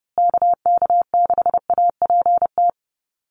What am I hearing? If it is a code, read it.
Morse code, heard as a keyed tone, about 30 wpm, 700 Hz. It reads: KK6APT
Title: USA Call Signs Repeated 2x 30wpm